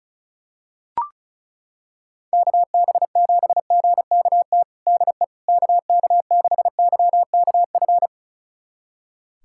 Morse code, heard as a keyed tone, about 36 wpm, 700 Hz. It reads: KB7GKT DE KK6YKF
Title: USA Callsign DE Callsign 35wpm